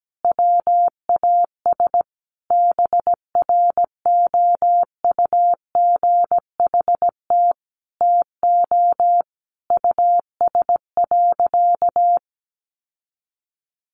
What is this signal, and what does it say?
Morse code, heard as a keyed tone, about 17 wpm, 700 Hz. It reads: WAS BROUGHT TO US.
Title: Sentences from Top 800 Words 17wpm